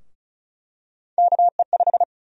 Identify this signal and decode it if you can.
Morse code, heard as a keyed tone, about 34 wpm, 700 Hz. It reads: KE5
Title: USA Call Sign Prefixes - 35wpm